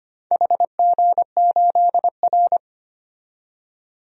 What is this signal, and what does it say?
Morse code, heard as a keyed tone, about 25 wpm, 700 Hz. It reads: HG8R
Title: DXCC-100 Call Signs - 4 Characters - 25wpm